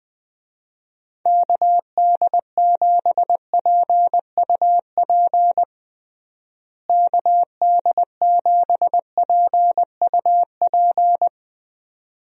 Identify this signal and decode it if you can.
Morse code, heard as a keyed tone, about 21 wpm, 700 Hz. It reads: KD7PUP KD7PUP
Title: USA Call Signs - 6 Characters - Repeated 2x - 20wpm